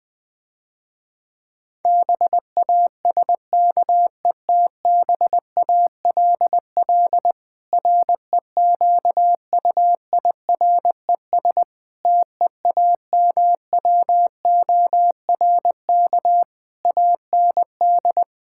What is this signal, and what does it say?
Morse code, heard as a keyed tone, about 20 wpm, 700 Hz. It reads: BASKETBALL REQUIRES TEAMWORK AND
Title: Difficult Sentences with Hints - 20wpm